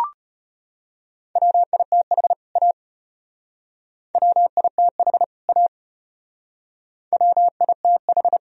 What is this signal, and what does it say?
Morse code, heard as a keyed tone, about 31 wpm, 700 Hz. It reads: WITH A WITH A WITH
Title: Speed-Racing - Sets of 2 Words 25wpm